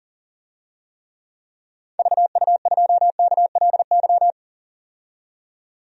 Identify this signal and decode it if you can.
Morse code, heard as a keyed tone, about 37 wpm, 700 Hz. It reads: VU2KLY
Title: DXCC-100 Call Signs - 6 Characters - 40wpm